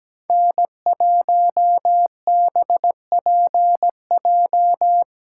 Morse code, heard as a keyed tone, about 17 wpm, 700 Hz. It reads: N1BPJ
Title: USA Call Signs - 5 Characters - 8x Character Spacing - 17wpm